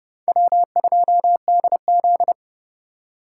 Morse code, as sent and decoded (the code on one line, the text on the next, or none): .-- ..--- -.. --..
W2DZ